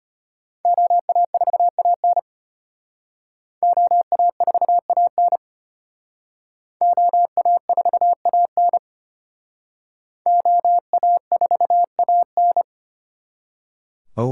Morse code, heard as a keyed tone, about 31 wpm, 700 Hz. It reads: OA4AN OA4AN OA4AN OA4AN
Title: Speed Racing - USA 2x2 Call Signs 25wpm